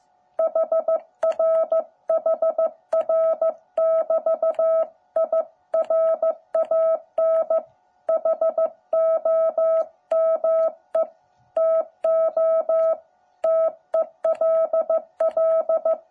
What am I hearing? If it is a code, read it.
Morse code, heard as a keyed tone, about 15 wpm, 650 Hz. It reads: HR HR <BT> I RAN HOME TO TELL